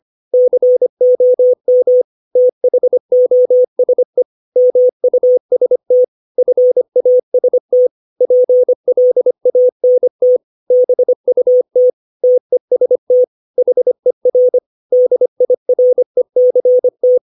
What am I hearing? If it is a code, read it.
Morse code, heard as a keyed tone, about 25 wpm, 500 Hz. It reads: COM THOSE MUST FAST PLANT BUT TEST HER DIRECT